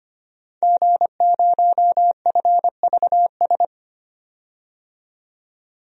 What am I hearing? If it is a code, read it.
Morse code, heard as a keyed tone, about 25 wpm, 700 Hz. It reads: G0FVS
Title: DXCC-100 Call Signs - 5 Characters - 25wpm